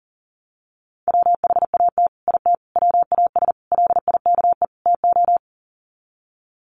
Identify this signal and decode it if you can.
Morse code, heard as a keyed tone, about 40 wpm, 700 Hz. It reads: WHAT IT WAS LIKE TO
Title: Sets of 5 Words Encore 40wpm